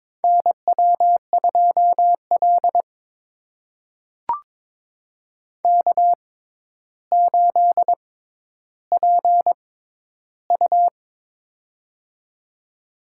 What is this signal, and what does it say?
Morse code, heard as a keyed tone, about 22 wpm, spaced as for 11 wpm, 700 Hz. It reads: NW2L K8PU
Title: USA Call Signs - 4 Characters - 6x Character Spacing - 22wpm